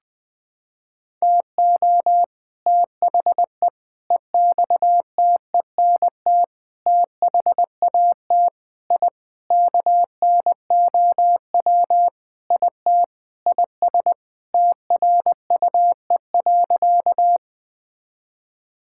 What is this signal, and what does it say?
Morse code, heard as a keyed tone, about 21 wpm, 700 Hz. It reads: TO THE EXTENT THAT I KNOW IT IS TRUE.
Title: Sentences with 4-Word Phrases - 20wpm